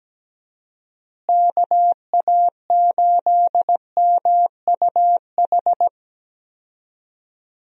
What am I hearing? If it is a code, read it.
Morse code, heard as a keyed tone, about 17 wpm, 700 Hz. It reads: KA8MUH
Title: USA Call Signs - 6 Characters - Repeated 3x - 17wpm